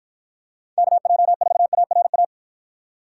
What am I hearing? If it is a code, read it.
Morse code, heard as a keyed tone, about 52 wpm, 700 Hz. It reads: KJ4ARA